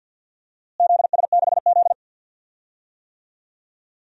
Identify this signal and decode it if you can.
Morse code, heard as a keyed tone, about 48 wpm, 700 Hz. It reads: ZS6Z
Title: DXCC-100 Call Signs - 4 Characters - Repeated 3x - 50wpm